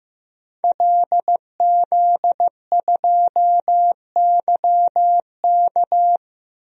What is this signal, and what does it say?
Morse code, heard as a keyed tone, about 15 wpm, 700 Hz. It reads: LZ2YK
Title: DXCC-100 Call Signs - 5 Characters - Repeated 2x - 15wpm